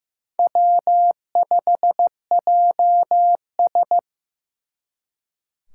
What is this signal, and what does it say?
Morse code, heard as a keyed tone, about 18 wpm, 700 Hz. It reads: W5JS
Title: Speed Racing - USA 1x2 Call Signs 15wpm